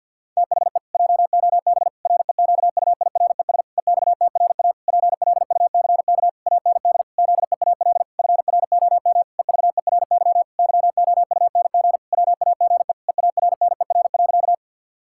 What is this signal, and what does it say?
Morse code, heard as a keyed tone, about 50 wpm, 700 Hz. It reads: THE JOB REQUIRES EXTRA PLUCK AND ZEAL FROM EVERY YOUNG WAGE EARNER.
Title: Pangram Challenge v2 - 50wpm